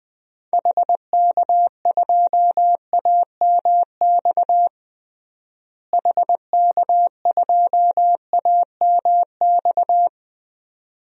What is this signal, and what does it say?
Morse code, heard as a keyed tone, about 20 wpm, 700 Hz. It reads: HK2AMX HK2AMX
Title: DXCC-100 Call Signs - 6 Characters - Repeated 2x - 20wpm